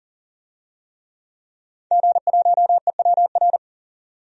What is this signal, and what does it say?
Morse code, heard as a keyed tone, about 40 wpm, 700 Hz. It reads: G1EWR
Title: DXCC-100 Call Signs - 5 Characters - 40wpm